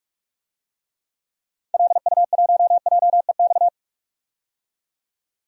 Morse code, heard as a keyed tone, about 45 wpm, 700 Hz. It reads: LU1JEX
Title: DXCC-100 Call Signs - 6 Characters - 45wpm